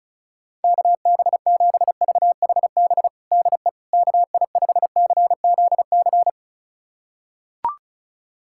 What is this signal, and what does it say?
Morse code, heard as a keyed tone, about 34 wpm, 700 Hz. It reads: KB7VHB DE KI5CZC
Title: USA Callsign DE Callsign - 6 Characters - 35wpm